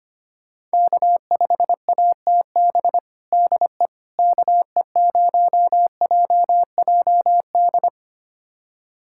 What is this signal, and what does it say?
Morse code, heard as a keyed tone, about 25 wpm, 700 Hz. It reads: K5ATB DE KE0JJD
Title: USA Callsign DE Callsign 25wpm